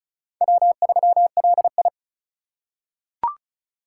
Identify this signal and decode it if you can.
Morse code, heard as a keyed tone, about 36 wpm, 700 Hz. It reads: W3LI
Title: USA Call Signs - 3x Character Spacing - 35wpm